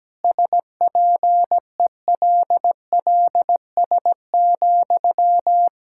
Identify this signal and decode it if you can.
Morse code, heard as a keyed tone, about 17 wpm, 700 Hz. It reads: SPELLS,
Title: Pangram Challenge v2 - 17wpm